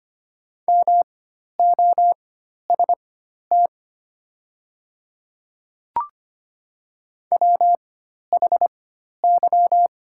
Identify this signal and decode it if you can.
Morse code, heard as a keyed tone, about 25 wpm, spaced as for 12 wpm, 700 Hz. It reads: MOST WHY
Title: 4x Character Spacing - Top 200 Words - 25wpm